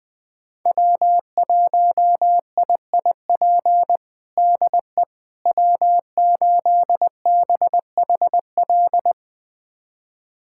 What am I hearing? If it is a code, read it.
Morse code, heard as a keyed tone, about 21 wpm, 700 Hz. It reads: W1IIP DE W8BHL
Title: USA Callsign DE Callsign - 5 Characters - 20wpm